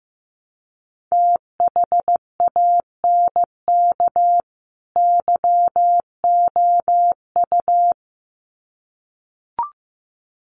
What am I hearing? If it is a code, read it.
Morse code, heard as a keyed tone, about 15 wpm, 700 Hz. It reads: THANK YOU
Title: Sets of 2 Words 15wpm